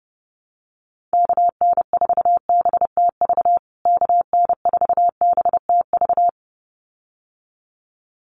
Morse code, heard as a keyed tone, about 30 wpm, 700 Hz. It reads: KN4BTV KN4BTV
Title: USA Call Signs Repeated 2x 30wpm